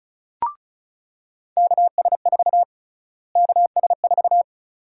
Morse code, heard as a keyed tone, about 36 wpm, 700 Hz. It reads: KS4 KS4
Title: USA Call Sign Prefixes - Repeated 2x - 35wpm